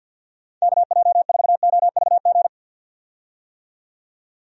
Morse code, heard as a keyed tone, about 48 wpm, 700 Hz. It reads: KJ4OVG
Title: USA Call Signs - 6 Characters - 50wpm